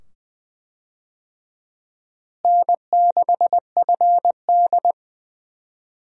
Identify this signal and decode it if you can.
Morse code, heard as a keyed tone, about 20 wpm, 700 Hz. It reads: N6FD
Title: USA Call Signs - 4 Characters - Repeated 2x - 20wpm